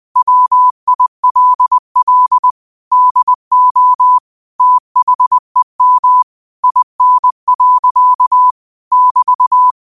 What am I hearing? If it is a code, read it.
Morse code, heard as a keyed tone, about 20 wpm, 1000 Hz. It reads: WILL DO THEM IN. <BT>
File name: qotdfiles / 20wpm